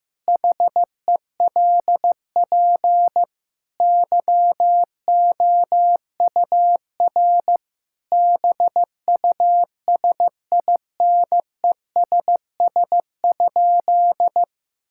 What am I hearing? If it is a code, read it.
Morse code, heard as a keyed tone, about 15 wpm, 700 Hz. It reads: HELP YOUR BUSINESS?
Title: Sentences from Top 1000 Words - Review Part 5 - 15wpm